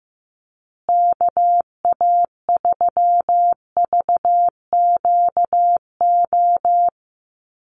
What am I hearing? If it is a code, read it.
Morse code, heard as a keyed tone, about 15 wpm, 700 Hz. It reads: KA3VQO
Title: USA Call Signs Repeated 2x 15wpm